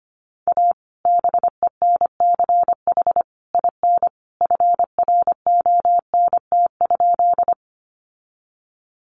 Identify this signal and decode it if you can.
Morse code, heard as a keyed tone, about 25 wpm, 700 Hz. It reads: A BENCH IN FRONT?
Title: Sentences from Top 1400 Words 25wpm